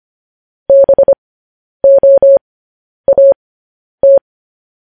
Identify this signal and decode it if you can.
Morse code, heard as a keyed tone, about 25 wpm, spaced as for 10 wpm, 550 Hz. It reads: BOAT